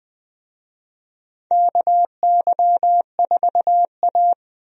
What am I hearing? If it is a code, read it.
Morse code, heard as a keyed tone, about 20 wpm, 700 Hz. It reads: KY4A
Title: USA Call Signs - 4 Characters - 20wpm